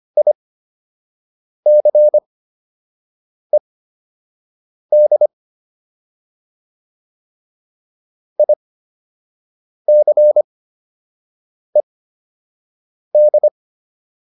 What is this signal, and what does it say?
Morse code, heard as a keyed tone, about 25 wpm, spaced as for 6 wpm, 600 Hz. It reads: ICED ICED